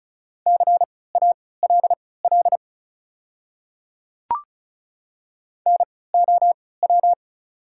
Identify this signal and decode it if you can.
Morse code, heard as a keyed tone, about 36 wpm, spaced as for 20 wpm, 700 Hz. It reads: CALL NOW
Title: 3x Character Spacing - Top 100 Words - 35wpm